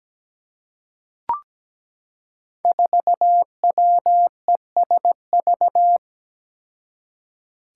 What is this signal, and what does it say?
Morse code, heard as a keyed tone, about 17 wpm, 700 Hz. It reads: 4WESV
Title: Five Challenging Letter-Number Combinations 17wpm